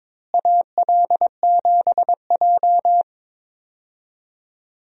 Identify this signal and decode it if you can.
Morse code, heard as a keyed tone, about 22 wpm, 700 Hz. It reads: AL7J
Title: USA Call Signs - 4 Characters - Repeated 2x - 22wpm